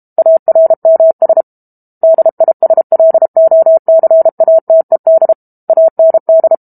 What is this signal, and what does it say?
Morse code, heard as a keyed tone, about 32 wpm, 650 Hz. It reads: ARMS DISLOCATED AND
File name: Book Two / 06-Chapter 6 - 32 WPM